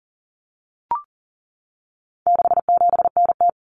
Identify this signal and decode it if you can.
Morse code, heard as a keyed tone, about 40 wpm, 700 Hz. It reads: 67NT
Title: Common Cut Numbers - 4-Digit 40wpm